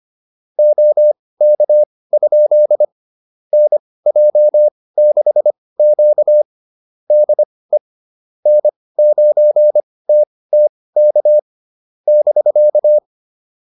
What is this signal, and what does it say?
Morse code, heard as a keyed tone, about 25 wpm, spaced as for 18 wpm, 600 Hz. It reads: OK? NJ6Q DE N9TTK <BK>